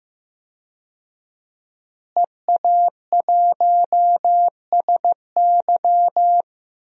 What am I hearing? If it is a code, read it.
Morse code, heard as a keyed tone, about 15 wpm, 700 Hz. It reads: EA1SY
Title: DXCC-100 Call Signs - 5 Characters - Repeated 3x - 15wpm